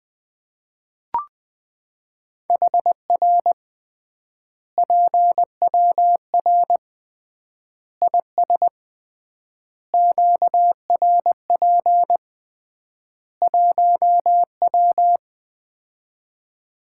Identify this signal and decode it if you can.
Morse code, heard as a keyed tone, about 21 wpm, 700 Hz. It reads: HR PWR IS QRP 1W
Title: Common QSO Phrases - 3x Word-Spacing - 20wpm